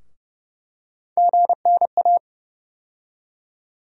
Morse code, heard as a keyed tone, about 30 wpm, 700 Hz. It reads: GNA
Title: USA Call Sign Postfixes - 3 Letters 30wpm